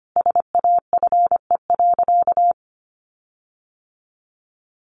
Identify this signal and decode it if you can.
Morse code, heard as a keyed tone, about 25 wpm, 700 Hz. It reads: SAFE.
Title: Sentences from Top 1300 Words 25wpm